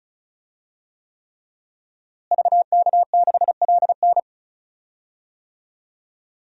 Morse code, heard as a keyed tone, about 36 wpm, 700 Hz. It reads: VK6LN
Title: DXCC-100 Call Signs - 5 Characters - Repeated 3x - 35wpm